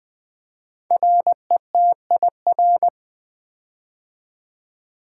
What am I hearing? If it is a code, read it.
Morse code, heard as a keyed tone, about 20 wpm, 700 Hz. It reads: RETIR
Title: Build A Word - 5 Letters At A Time - 10-Letter Words - 20wpm